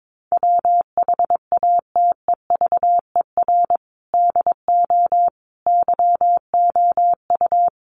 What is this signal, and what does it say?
Morse code, heard as a keyed tone, about 22 wpm, 700 Hz. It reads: WHATEVER DO YOU
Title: Sentences from Top 1100 Words 22wpm